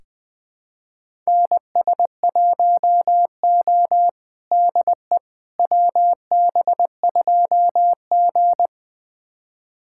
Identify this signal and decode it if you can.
Morse code, heard as a keyed tone, about 20 wpm, 700 Hz. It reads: NS1O DE WB2G
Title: USA Callsign DE Callsign - 4 Characters - 20wpm